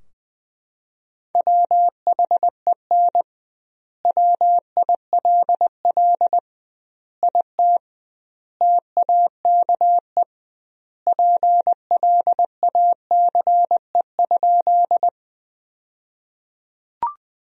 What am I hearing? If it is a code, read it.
Morse code, heard as a keyed tone, about 20 wpm, 700 Hz. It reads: WHEN WILL IT TAKE PLACE?
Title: 2x Word-Spacing - Sentences from Top 200 Words - 20wpm